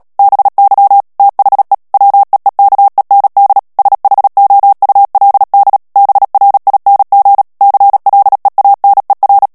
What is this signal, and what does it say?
Morse code, heard as a keyed tone, about 37 wpm, 800 Hz. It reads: BY THE WEEKEND SHOULD BRING CLEANER
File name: morse mp3 / ANGLAIS texte clair 2 vitesse 35 mots minute